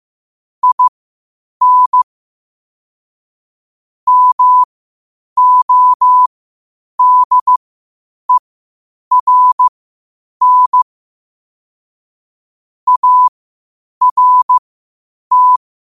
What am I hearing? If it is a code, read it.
Morse code, heard as a keyed tone, about 15 wpm, spaced as for 8 wpm, 1000 Hz. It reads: IN MODERN ART
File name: qotdfiles / 8wpm